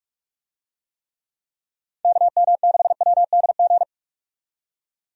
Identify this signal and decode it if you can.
Morse code, heard as a keyed tone, about 44 wpm, 700 Hz. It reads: KM6WDG
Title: USA Call Signs - 6 Characters - 45wpm